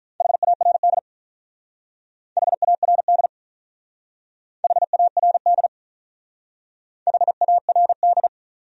Morse code, heard as a keyed tone, about 42 wpm, 700 Hz. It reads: HARD HARD HARD HARD
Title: Speed-Racing Top 500 Words 35wpm